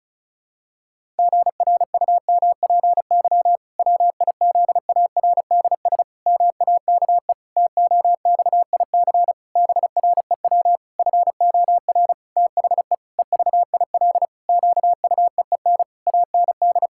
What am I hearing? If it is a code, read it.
Morse code, heard as a keyed tone, about 36 wpm, 700 Hz. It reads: GRUMPY WIZARDS MAKE TOXIC BREW FOR THE EVIL QUEEN AND
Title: Pangram Challenge v2 - 35wpm